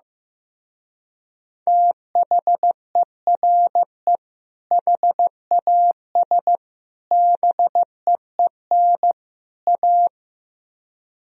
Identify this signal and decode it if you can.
Morse code, heard as a keyed tone, about 15 wpm, 700 Hz. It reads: THERE HAS BEEN A
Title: Speed-Racing - Sets of 4 Words 15wpm